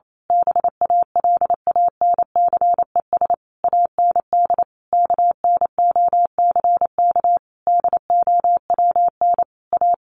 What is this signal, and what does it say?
Morse code, heard as a keyed tone, about 28 wpm, 700 Hz. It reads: BALANCES AND KNOCK DOWN A
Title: A Sound Of Thunder 28wpm